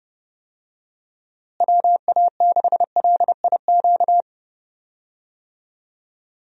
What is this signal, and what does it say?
Morse code, heard as a keyed tone, about 32 wpm, 700 Hz. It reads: WA6LIQ
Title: USA Call Signs - 6 Characters - Repeated 2x - 30wpm